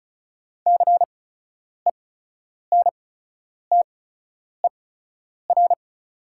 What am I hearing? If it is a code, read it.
Morse code, heard as a keyed tone, about 36 wpm, spaced as for 10 wpm, 690 Hz. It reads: CENTER
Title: 8x Character Spacing - Top 500 Words - 35wpm